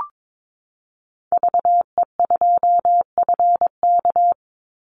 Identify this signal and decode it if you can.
Morse code, heard as a keyed tone, about 22 wpm, 700 Hz. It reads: VE2FK
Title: Call Signs - 22wpm